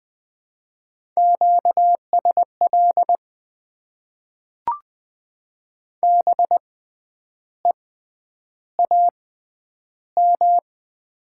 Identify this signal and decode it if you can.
Morse code, heard as a keyed tone, about 20 wpm, spaced as for 9 wpm, 700 Hz. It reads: QSL BEAM
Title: Top 90 QSO Elements - 6x Character Spacing - 20wpm